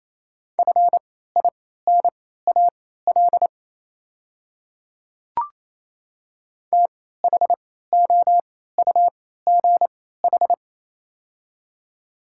Morse code, heard as a keyed tone, about 27 wpm, spaced as for 16 wpm, 700 Hz. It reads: FINAL THOUGH
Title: 3x Character Spacing - Top 500 Words - 28wpm